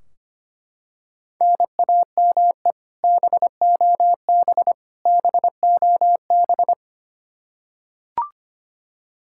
Morse code, heard as a keyed tone, about 25 wpm, 700 Hz. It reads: NAME BOB BOB
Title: K1USN SST QSO Practice 25wpm